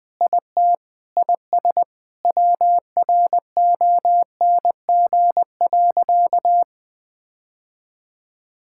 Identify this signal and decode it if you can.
Morse code, heard as a keyed tone, about 20 wpm, 700 Hz. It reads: IT IS WRONG.
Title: Sentences with 7-Word Phrases - 20wpm